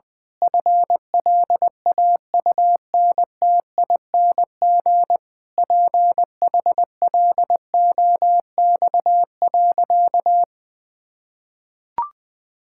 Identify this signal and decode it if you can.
Morse code, heard as a keyed tone, about 20 wpm, 700 Hz. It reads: FLAUNTING PHLOX.
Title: Pangram Challenge v2 - 20wpm